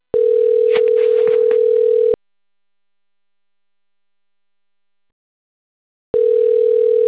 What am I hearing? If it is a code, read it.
Ringback tone (North America), 440 + 480 Hz.